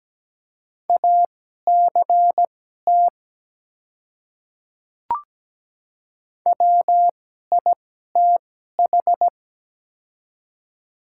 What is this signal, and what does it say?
Morse code, heard as a keyed tone, about 17 wpm, spaced as for 12 wpm, 700 Hz. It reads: ACT WITH
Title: 2x Character Spacing - Top 200 Words - 17wpm